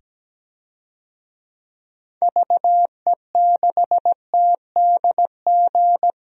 Morse code, heard as a keyed tone, about 17 wpm, 700 Hz. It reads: VE6TDG
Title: DXCC-100 Call Signs - 6 Characters - 17wpm